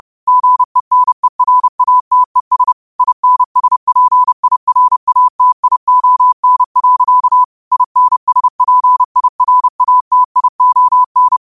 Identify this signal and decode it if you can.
Morse code, heard as a keyed tone, about 30 wpm, 1000 Hz. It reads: GENERATES INSPIRATION. INSPIRATION
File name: qotdfiles / 30wpm